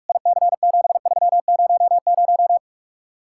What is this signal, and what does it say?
Morse code, heard as a keyed tone, about 54 wpm, 700 Hz. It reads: IC7300
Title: Speed-Racing Common QSO Phrases - 45wpm